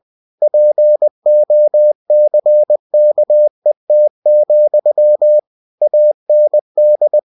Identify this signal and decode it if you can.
Morse code, heard as a keyed tone, about 20 wpm, 600 Hz. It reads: POCKET, AND
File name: cw-books / winnie-the-pooh / winnie-20wpm-page-0079